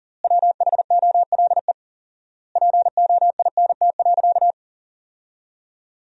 Morse code, heard as a keyed tone, about 40 wpm, 700 Hz. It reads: WHOLE POINT.
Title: 4x Word-Spacing - Sentences from Top 400 Words - 40wpm